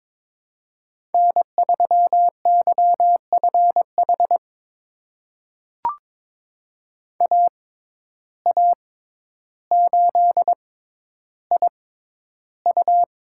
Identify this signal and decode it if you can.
Morse code, heard as a keyed tone, about 22 wpm, spaced as for 11 wpm, 700 Hz. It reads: N3YFH AA8IU
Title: USA Call Signs - 5 Characters - 6x Character Spacing - 22wpm